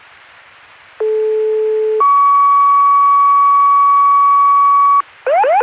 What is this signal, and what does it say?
Two-tone page, heard as a one-second tone, then a three-second tone, about 430 then 1130 Hz. It is heard from 1.0 to 5.0 s.